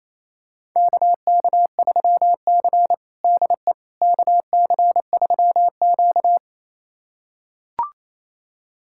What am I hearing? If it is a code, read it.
Morse code, heard as a keyed tone, about 28 wpm, 700 Hz. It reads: KK3C DE KC3Q
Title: USA Callsign DE Callsign - 4 Characters - 28wpm